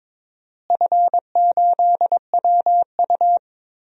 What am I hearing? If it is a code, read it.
Morse code, heard as a keyed tone, about 22 wpm, 700 Hz. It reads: F8WU
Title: DXCC-100 Call Signs - 4 Characters - Repeated 2x - 22wpm